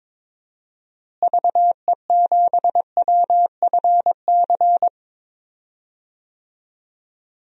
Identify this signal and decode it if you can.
Morse code, heard as a keyed tone, about 22 wpm, 700 Hz. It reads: VE7WFC
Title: DXCC-100 Call Signs - 6 Characters - Repeated 3x - 22wpm